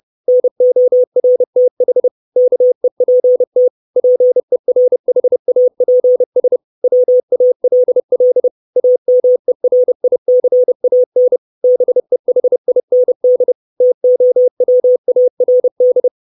Morse code, heard as a keyed tone, about 30 wpm, 500 Hz. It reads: NORTH KEPT PERHAPS WALL AMERICAN BEHIND TOWARD
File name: 632 Words 30WPM